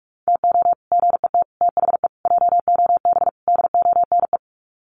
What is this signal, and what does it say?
Morse code, heard as a keyed tone, about 44 wpm, 700 Hz. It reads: TO GET THE JOB DONE
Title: Sets of 5 Words Encore2 45wpm